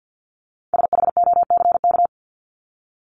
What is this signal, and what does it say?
Morse code, heard as a keyed tone, about 49 wpm, 700 Hz. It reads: SHOCK
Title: Top 1751-2000 Words 50wpm